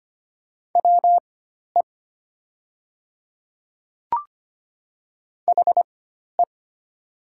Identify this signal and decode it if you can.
Morse code, heard as a keyed tone, about 25 wpm, spaced as for 12 wpm, 700 Hz. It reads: WE HE
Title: 4x Character Spacing - Top 100 Words - 25wpm